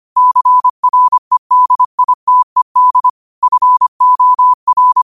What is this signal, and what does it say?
Morse code, heard as a keyed tone, about 25 wpm, 1000 Hz. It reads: CREDITED FOR